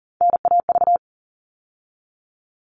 Morse code, heard as a keyed tone, about 40 wpm, 700 Hz. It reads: NAV
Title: All Cut Numbers - 3-Digit 40wpm